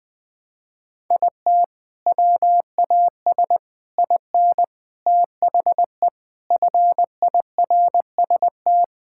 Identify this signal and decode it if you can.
Morse code, heard as a keyed tone, about 20 wpm, 700 Hz. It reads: IT WAS IN THE FIRST
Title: Sentences from Top 1000 Words - Review Part 5 - 20wpm